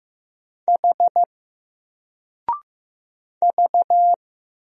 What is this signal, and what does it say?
Morse code, heard as a keyed tone, about 15 wpm, 700 Hz. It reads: H V
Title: Single Letter-Number 15wpm